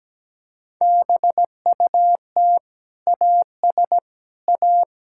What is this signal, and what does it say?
Morse code, heard as a keyed tone, about 17 wpm, 700 Hz. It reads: BUT AS A
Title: Sentences from Top 1000 Words - Review Part 6 - 17wpm